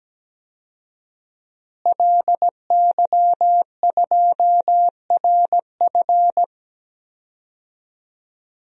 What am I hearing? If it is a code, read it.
Morse code, heard as a keyed tone, about 17 wpm, 700 Hz. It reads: LY2RF
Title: DXCC-100 Call Signs - 5 Characters - Repeated 2x - 17wpm